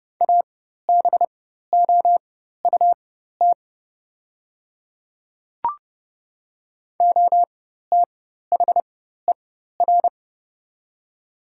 Morse code, heard as a keyed tone, about 30 wpm, spaced as for 14 wpm, 700 Hz. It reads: ABOUT OTHER
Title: 4x Character Spacing - Top 100 Words - 30wpm